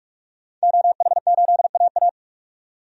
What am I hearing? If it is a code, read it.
Morse code, heard as a keyed tone, about 44 wpm, 700 Hz. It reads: OH8AA
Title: DXCC-100 Call Signs - 5 Characters - 45wpm